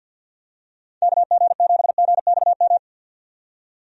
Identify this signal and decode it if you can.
Morse code, heard as a keyed tone, about 50 wpm, 700 Hz. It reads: KG7GXM
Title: USA Call Signs - 6 Characters - 50wpm